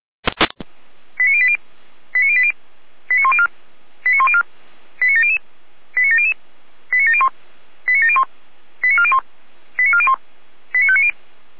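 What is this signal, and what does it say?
Five-tone selective call (ZVEI), heard as quick runs of five tones, tones of about 70 ms. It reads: 89080 89080 89104 89104 89700 89700 89701 89701 89401 89401 89590
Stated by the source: ZVEI five-tone call 89094 received on SDR.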